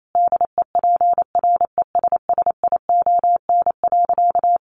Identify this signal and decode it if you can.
Morse code, heard as a keyed tone, about 28 wpm, 700 Hz. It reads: DEPRESSION.
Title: Sentences from Top 1300 Words 28wpm